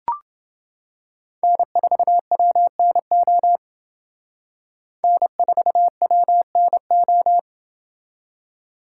Morse code, heard as a keyed tone, about 28 wpm, 700 Hz. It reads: N4WNO N4WNO